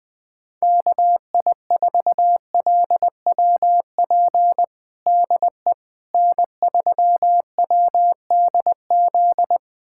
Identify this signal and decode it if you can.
Morse code, heard as a keyed tone, about 20 wpm, 700 Hz. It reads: KI4LWP DE N3WDZ
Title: USA Callsign DE Callsign 20wpm